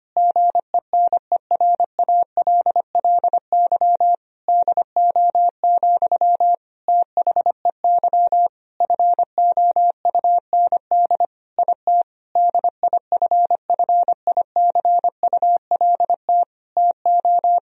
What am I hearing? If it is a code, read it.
Morse code, heard as a keyed tone, about 25 wpm, 700 Hz. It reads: GENERALLY DO, THEY FOUND IT DIFFICULT TO
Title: Sentences from Top 1000 Words - Review Part 4 - 25wpm